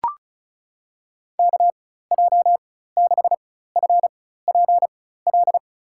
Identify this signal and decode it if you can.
Morse code, heard as a keyed tone, about 36 wpm, spaced as for 16 wpm, 700 Hz. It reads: KJ6FPL
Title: USA Call Signs - 6 Characters - 4x Character Spacing - 35wpm